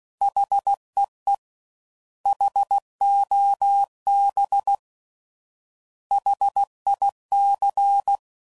Morse code, heard as a keyed tone, about 16 wpm, 800 Hz. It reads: HEE HOB HIC